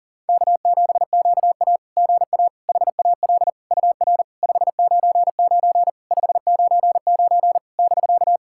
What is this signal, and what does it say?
Morse code, heard as a keyed tone, about 37 wpm, 690 Hz. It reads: K7QA GA HAL UR 599 599 <BK>
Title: SOTA QSO Practice 40wpm